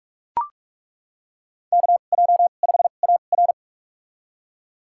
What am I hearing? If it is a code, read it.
Morse code, heard as a keyed tone, about 41 wpm, spaced as for 31 wpm, 700 Hz. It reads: KJ5AR
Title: USA Call Signs - 2x Character Spacing - 45wpm